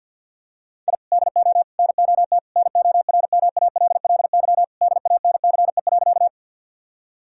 Morse code, heard as a keyed tone, about 50 wpm, 700 Hz. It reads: I DO NOT NORMALLY DANCE.
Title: Sentences from Top 1000 Words - Review Part 6 - 50wpm